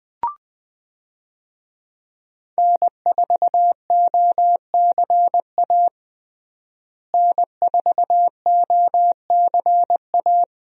Morse code, heard as a keyed tone, about 20 wpm, 700 Hz. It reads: N4OCA N4OCA
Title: USA Call Signs - 5 Characters - Repeated 3x - 20wpm